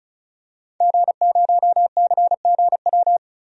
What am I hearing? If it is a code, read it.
Morse code, heard as a keyed tone, about 35 wpm, 700 Hz. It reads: G0CGW
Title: DXCC-100 Call Signs - 5 Characters - 35wpm